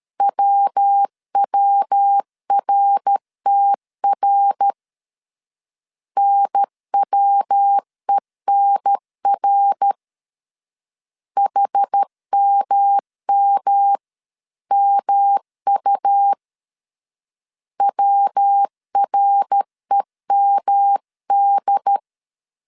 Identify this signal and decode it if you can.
Morse code, heard as a keyed tone, about 13 wpm, 800 Hz. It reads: WWRTR NWENR HMMMU WREMD
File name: CW / DET / 1 3 1500C S B 12wpm 2828